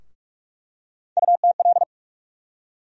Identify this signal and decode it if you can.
Morse code, heard as a keyed tone, about 45 wpm, 700 Hz. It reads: UTL
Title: USA Call Sign Postfixes - 3 Letters 45wpm